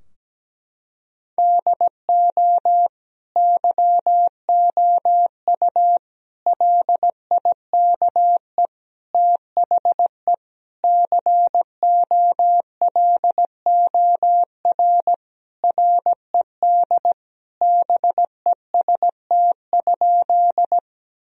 Morse code, heard as a keyed tone, about 17 wpm, 700 Hz. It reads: DO YOU LIKE THE COLOR RED BEST?
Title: Sentences from Top 1000 Words - Review Part 2 - 17wpm